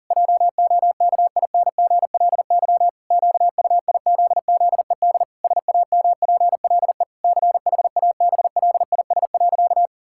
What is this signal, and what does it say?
Morse code, heard as a keyed tone, about 40 wpm, 700 Hz. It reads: JOKINGLY QUIZZED SAMPLE CHABLIS.
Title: Pangram Challenge v2 - 40wpm